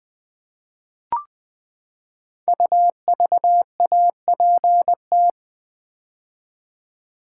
Morse code, heard as a keyed tone, about 20 wpm, 700 Hz. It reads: UVAPT
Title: Code Groups - 5 Letter:Numbers - 20wpm